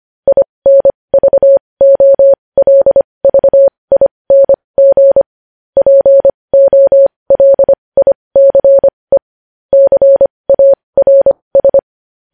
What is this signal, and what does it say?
Morse code, heard as a keyed tone, about 25 wpm, spaced as for 20 wpm, 550 Hz. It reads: INVOLVING POLICE CARS
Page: news headlines in morse code at 20 wpm